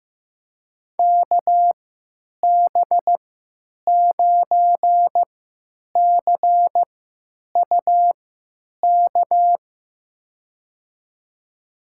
Morse code, heard as a keyed tone, about 15 wpm, spaced as for 9 wpm, 700 Hz. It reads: KB9CUK
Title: USA Call Signs - 3x Character Spacing - 15wpm